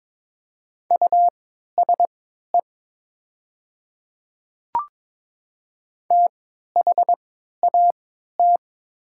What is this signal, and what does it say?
Morse code, heard as a keyed tone, about 21 wpm, spaced as for 12 wpm, 700 Hz. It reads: USE THAT
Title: 3x Character Spacing - Top 100 Words - 22wpm